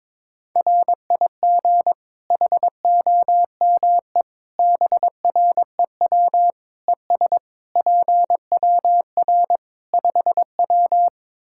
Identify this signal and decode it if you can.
Morse code, heard as a keyed tone, about 22 wpm, 700 Hz. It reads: RIG HOME BREW ES PWR 5W
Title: Practice QSO - Exchange 2 - Info - 22wpm